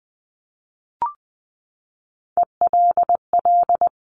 Morse code, heard as a keyed tone, about 20 wpm, 700 Hz. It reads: ELL
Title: Top Three Letter Combinations in Words 20wpm